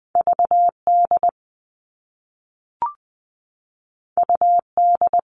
Morse code, heard as a keyed tone, about 20 wpm, 700 Hz. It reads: VD UD
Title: All Cut Numbers - 2-Digit 20wpm